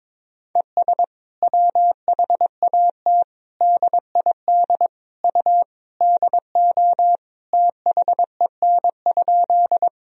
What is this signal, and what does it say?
Morse code, heard as a keyed tone, about 22 wpm, 700 Hz. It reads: ES WHAT DID U DO THEN?
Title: Sentences from Top 100 Words and Abbreviations 22wpm